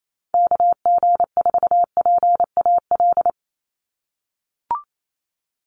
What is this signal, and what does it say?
Morse code, heard as a keyed tone, about 28 wpm, 700 Hz. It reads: KG4PAL
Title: USA Call Signs 28wpm